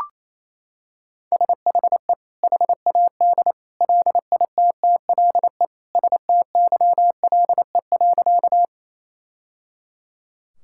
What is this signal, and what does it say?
Morse code, heard as a keyed tone, about 28 wpm, 700 Hz. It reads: SHE HAD LITTLE STYLE.
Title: Sentences from Top 1000 Words - Review Part 5 - 28wpm